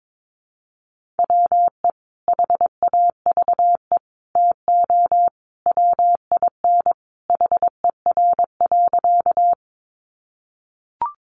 Morse code, heard as a keyed tone, about 22 wpm, 700 Hz. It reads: WE HAVE TO WIN HER.
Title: Sentences from Top 1500 Words 22wpm